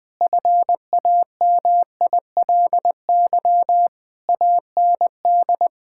Morse code, heard as a keyed tone, about 20 wpm, 700 Hz. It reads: FAMILY AND
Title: Sentences from Top 1000 Words - Review Part 5 - 20wpm